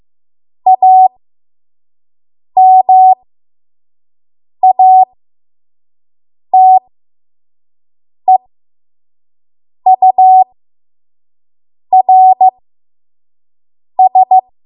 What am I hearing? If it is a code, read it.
Morse code, heard as a keyed tone, about 15 wpm, spaced as for 5 wpm, 750 Hz. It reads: AMATEURS
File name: Morse / 05WPM